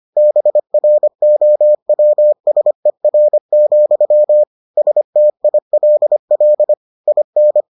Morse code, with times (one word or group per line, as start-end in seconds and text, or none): 0.2-4.4 BROWSER,
4.8-6.7 STILL
7.1-7.6 IN